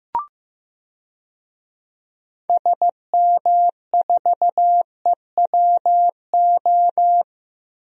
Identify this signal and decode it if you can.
Morse code, heard as a keyed tone, about 15 wpm, 700 Hz. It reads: SM4EWO
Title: DXCC-100 Call Signs - 6 Characters - 15wpm